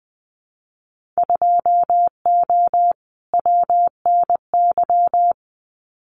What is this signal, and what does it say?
Morse code, heard as a keyed tone, about 20 wpm, 700 Hz. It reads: 2O WNY
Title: Winter Field Day Exchanges Encore 20wpm